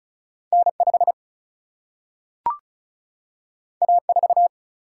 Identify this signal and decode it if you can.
Morse code, heard as a keyed tone, about 36 wpm, 700 Hz. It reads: N 5 A 4
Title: Single-Letter Single-Number 35wpm